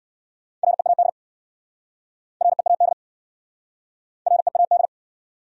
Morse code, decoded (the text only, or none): READ READ READ